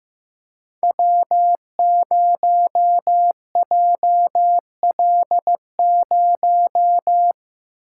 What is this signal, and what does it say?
Morse code, heard as a keyed tone, about 15 wpm, 700 Hz. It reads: W0JL0
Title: Code Groups - 5 Letter:Numbers - 15wpm